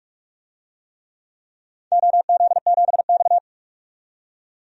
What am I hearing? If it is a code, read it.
Morse code, heard as a keyed tone, about 47 wpm, 700 Hz. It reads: OZ7X
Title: DXCC-100 Call Signs - 4 Characters - 45wpm